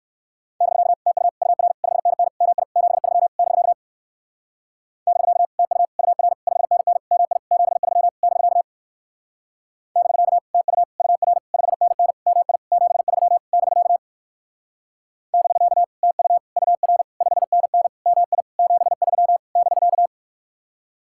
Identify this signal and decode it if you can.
Morse code, heard as a keyed tone, about 60 wpm, 700 Hz. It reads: <BK> TU UR 5NN MI 73 <BK> <BK> TU UR 5NN MI 73 <BK> <BK> TU UR 5NN MI 73 <BK> <BK> TU UR 5NN MI 73 <BK>